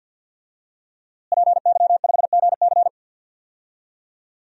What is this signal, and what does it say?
Morse code, heard as a keyed tone, about 51 wpm, 700 Hz. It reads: PY5GC
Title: DXCC-100 Call Signs - 5 Characters - Repeated 2x - 50wpm